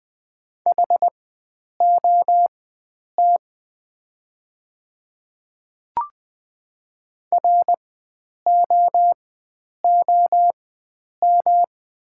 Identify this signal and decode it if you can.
Morse code, heard as a keyed tone, about 20 wpm, spaced as for 9 wpm, 700 Hz. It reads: HOT ROOM
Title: 4x Character Spacing - Top 300 Words - 20wpm